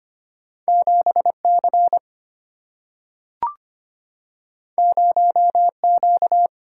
Morse code, heard as a keyed tone, about 25 wpm, 700 Hz. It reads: 7 C 0 Q
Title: Single-Letter Single-Number 25wpm